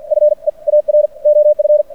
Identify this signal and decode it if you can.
Morse code, heard as a keyed tone, about 50 wpm, spaced as for 40 wpm, 590 Hz. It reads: FEAR OF
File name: archived / CRSnet